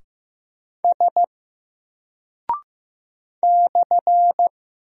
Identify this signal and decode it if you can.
Morse code, heard as a keyed tone, about 15 wpm, 700 Hz. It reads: S /
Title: Single Character (Letter:Number:Punctuation:Prosign) 15wpm